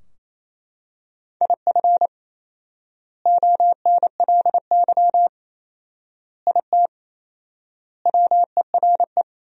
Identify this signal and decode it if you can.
Morse code, heard as a keyed tone, about 28 wpm, 700 Hz. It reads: IF ONLY IT WERE
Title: 4x Word-Spacing - Sentences from Top 500 Words - 28wpm